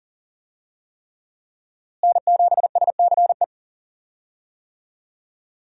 Morse code, heard as a keyed tone, about 40 wpm, 700 Hz. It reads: N7SCE
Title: USA Call Signs - 5 Characters - Repeated 3x - 40wpm